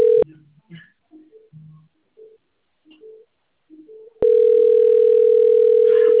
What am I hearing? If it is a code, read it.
Ringback tone (North America), 440 + 480 Hz.